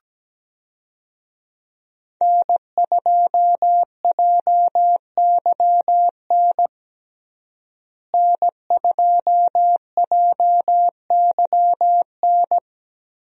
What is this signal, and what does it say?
Morse code, heard as a keyed tone, about 17 wpm, 700 Hz. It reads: N2JYN N2JYN
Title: USA Call Signs - 5 Characters - Repeated 3x - 17wpm